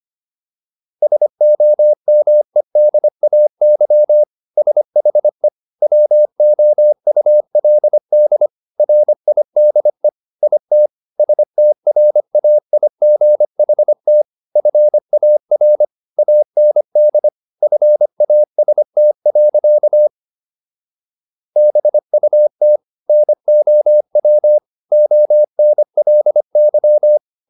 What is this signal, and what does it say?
Morse code, heard as a keyed tone, about 25 wpm, 600 Hz. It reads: SOMEDAY SHE WOULD RIDE IT STRAIGHT FAR AND FAST. BUT NOW ONLY